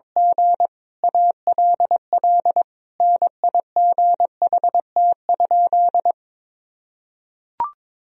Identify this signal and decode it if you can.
Morse code, heard as a keyed tone, about 21 wpm, 700 Hz. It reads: G ALL NIGHT?
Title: Sentences from Top 1000 Words - Review Part 4 - 22wpm